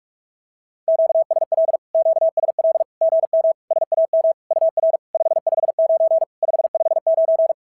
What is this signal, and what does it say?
Morse code, heard as a keyed tone, about 44 wpm, 660 Hz. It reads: QSL QSL GM SAM UR 559 559
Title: SOTA QSO Practice 45wpm